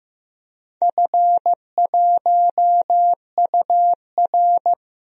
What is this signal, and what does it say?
Morse code, heard as a keyed tone, about 15 wpm, 700 Hz. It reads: F1UR
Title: DXCC-100 Call Signs - 4 Characters - 15wpm